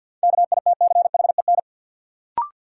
Morse code, heard as a keyed tone, about 53 wpm, 700 Hz. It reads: KITCHEN
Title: Sentences from Top 1400 Words - New Word Drill Alt - 50wpm